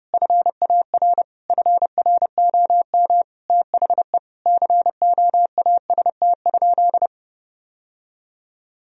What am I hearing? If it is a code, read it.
Morse code, heard as a keyed tone, about 32 wpm, 700 Hz. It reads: FAR FROM THE COAST?
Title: Sentences from Top 1000 Words - Review Part 4 - 30wpm